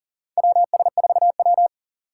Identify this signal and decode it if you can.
Morse code, heard as a keyed tone, about 39 wpm, 700 Hz. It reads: WS4W
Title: USA Call Signs - 4 Characters - 2x Character Spacing - 40wpm